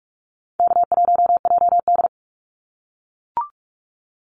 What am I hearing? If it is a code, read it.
Morse code, heard as a keyed tone, about 44 wpm, 700 Hz. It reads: K1JD
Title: NA SOTA Activator Callsigns 45wpm